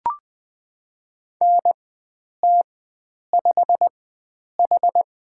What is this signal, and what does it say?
Morse code, heard as a keyed tone, about 20 wpm, spaced as for 9 wpm, 700 Hz. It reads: NT5H
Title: USA Call Signs - 4x Character Spacing - 20wpm